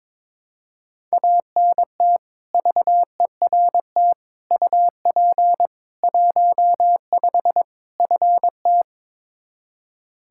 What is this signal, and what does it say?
Morse code, heard as a keyed tone, about 22 wpm, 700 Hz. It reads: ANT VERT UP 15 FT
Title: Practice QSO - Exchange 2 - Info - 22wpm